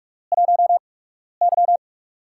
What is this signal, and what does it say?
Morse code, heard as a keyed tone, about 44 wpm, spaced as for 12 wpm, 700 Hz. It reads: 1Y
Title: USA Call Signs - 8x Character Spacing - 45wpm